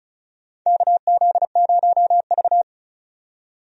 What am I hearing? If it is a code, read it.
Morse code, heard as a keyed tone, about 36 wpm, 700 Hz. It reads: KZ0V
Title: USA Call Signs - 4 Characters - 35wpm